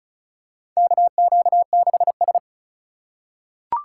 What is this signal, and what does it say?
Morse code, heard as a keyed tone, about 36 wpm, 700 Hz. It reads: KQ6S
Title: USA Call Signs - 4 Characters - 3x Character Spacing - 35wpm